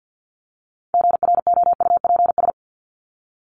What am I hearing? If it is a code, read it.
Morse code, heard as a keyed tone, about 50 wpm, 700 Hz. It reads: GROUPS
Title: ES:S Suffix - Top 100 Words with Base Words 50wpm